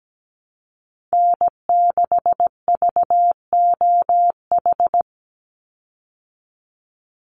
Morse code, heard as a keyed tone, about 17 wpm, 700 Hz. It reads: N6VOH
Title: Call Signs Repeated 3x - 17wpm